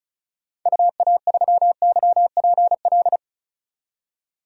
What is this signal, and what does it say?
Morse code, heard as a keyed tone, about 34 wpm, 700 Hz. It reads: UA3YPL
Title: DXCC-100 Call Signs - 6 Characters - 35wpm